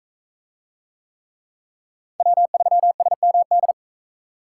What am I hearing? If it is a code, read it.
Morse code, heard as a keyed tone, about 41 wpm, 700 Hz. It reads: W3SMD